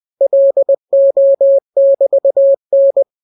Morse code, heard as a keyed tone, about 20 wpm, 550 Hz. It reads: LO<BT>N